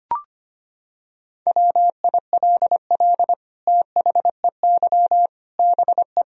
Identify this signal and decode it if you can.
Morse code, heard as a keyed tone, about 25 wpm, 700 Hz. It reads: WILL THEY BE
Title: Sentences from Top 1000 Words - Review Part 4 - 25wpm